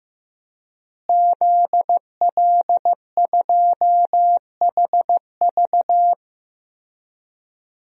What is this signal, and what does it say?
Morse code, heard as a keyed tone, about 15 wpm, 700 Hz. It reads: ZL2HV
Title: DXCC-100 Call Signs - 5 Characters - Repeated 3x - 15wpm